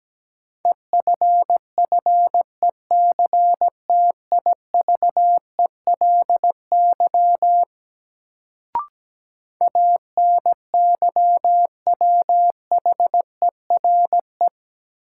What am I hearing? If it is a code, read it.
Morse code, heard as a keyed tone, about 17 wpm, 700 Hz. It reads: EFFECTIVELY ANYWHERE
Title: Sentences from Top 1400 Words - New Word Drill - 17wpm